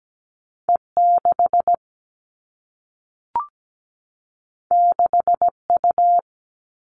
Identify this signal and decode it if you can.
Morse code, heard as a keyed tone, about 17 wpm, 700 Hz. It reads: E6 6U
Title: All Cut Numbers - 2-Digit 17wpm